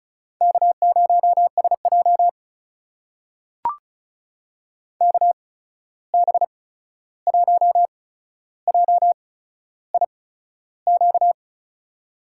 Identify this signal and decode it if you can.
Morse code, heard as a keyed tone, about 36 wpm, spaced as for 13 wpm, 700 Hz. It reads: K0SJ KB1JIQ
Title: USA Call Signs - 8x Character Spacing - 35wpm